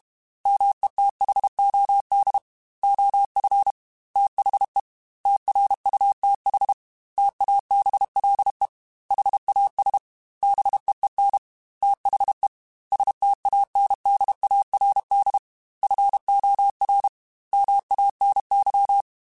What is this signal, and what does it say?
Morse code, heard as a keyed tone, about 32 wpm, 800 Hz. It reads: METHOD OF THE TRUTH TABLE HAS BEEN THE STANDARD FOR MANY
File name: Technical Subjects / Technical Subjects-mp3s / AND Gate @30